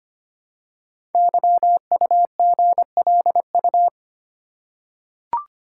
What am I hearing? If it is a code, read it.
Morse code, heard as a keyed tone, about 25 wpm, 700 Hz. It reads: YUGLU
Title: Code Groups - 5 Letter:Numbers - 25wpm